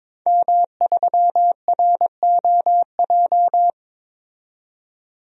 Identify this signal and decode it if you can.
Morse code, heard as a keyed tone, about 22 wpm, 700 Hz. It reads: M3ROJ
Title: DXCC-100 Call Signs - 5 Characters - 22wpm